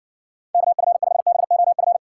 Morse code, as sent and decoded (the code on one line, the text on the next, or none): -.- ..-. ..... -... --- ..-.
KF5BOF